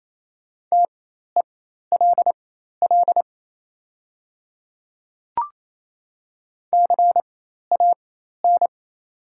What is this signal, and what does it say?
Morse code, heard as a keyed tone, about 27 wpm, spaced as for 13 wpm, 700 Hz. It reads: TELL CAN
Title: 4x Character Spacing - Top 500 Words - 28wpm